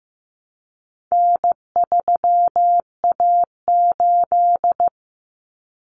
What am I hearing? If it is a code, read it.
Morse code, heard as a keyed tone, about 15 wpm, 700 Hz. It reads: N3A8
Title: Common Cut Numbers - 4-Digit 15wpm